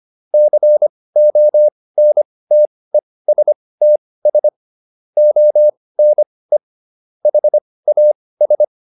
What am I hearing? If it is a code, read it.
Morse code, heard as a keyed tone, about 25 wpm, spaced as for 18 wpm, 600 Hz. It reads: CONTESTS ONE HAS